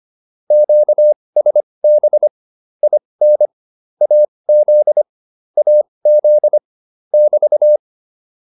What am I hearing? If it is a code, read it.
Morse code, heard as a keyed tone, about 25 wpm, spaced as for 20 wpm, 600 Hz. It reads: QSB IN AZ AZ <BT>